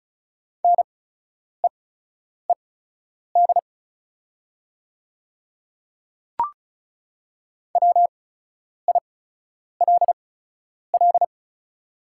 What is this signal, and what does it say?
Morse code, heard as a keyed tone, about 36 wpm, spaced as for 10 wpm, 700 Hz. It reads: NEED WILL
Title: 8x Character Spacing - Top 200 Words - 35wpm